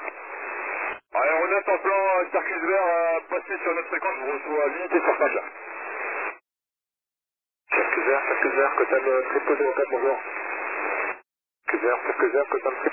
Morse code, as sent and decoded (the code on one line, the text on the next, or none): none